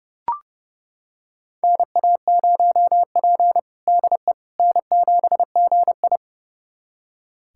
Morse code, heard as a keyed tone, about 32 wpm, 700 Hz. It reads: NA0P DE N7GI
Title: USA Callsign DE Callsign - 4 Characters - 30wpm